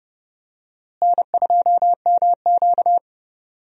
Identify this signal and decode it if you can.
Morse code, heard as a keyed tone, about 30 wpm, 700 Hz. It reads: N2MQ
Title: USA Call Signs - 4 Characters - Repeated 3x - 30wpm